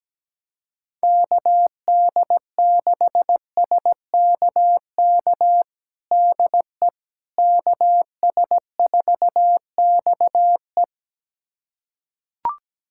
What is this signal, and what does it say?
Morse code, heard as a keyed tone, about 17 wpm, 700 Hz. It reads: KD6SKK DE KS4XE
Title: USA Callsign DE Callsign 17wpm